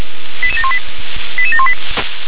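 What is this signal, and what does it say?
Five-tone selective call (ZVEI), heard as quick runs of five tones, tones of about 70 ms. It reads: 88718 88718
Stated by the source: SDR-received ZVEI five-tone selective call, address 88711.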